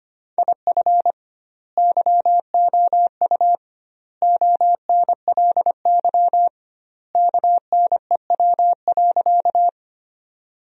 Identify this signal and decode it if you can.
Morse code, heard as a keyed tone, about 25 wpm, 700 Hz. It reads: IF YOU ONLY KNEW.
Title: 2x Word-Spacing - Sentences from Top 400 Words - 25wpm